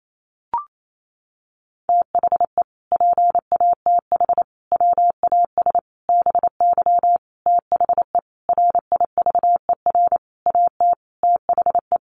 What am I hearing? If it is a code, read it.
Morse code, heard as a keyed tone, about 28 wpm, 700 Hz. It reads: THE PATH WAS BY THE RIVER AT THE
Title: Sentences from Top 1400 Words 28wpm